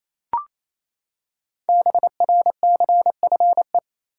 Morse code, heard as a keyed tone, about 29 wpm, 700 Hz. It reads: BRCFE
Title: Code Groups - 5 Letters - 28wpm